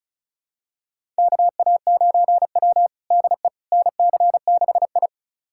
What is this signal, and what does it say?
Morse code, heard as a keyed tone, about 36 wpm, 700 Hz. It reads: KA9W DE NC6I
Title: USA Callsign DE Callsign - 4 Characters - 35wpm